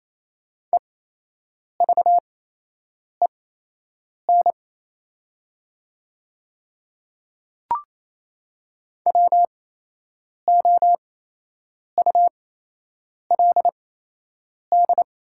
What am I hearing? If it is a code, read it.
Morse code, heard as a keyed tone, about 28 wpm, spaced as for 8 wpm, 700 Hz. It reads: EVEN WOULD
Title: 8x Character Spacing - Top 200 Words - 28wpm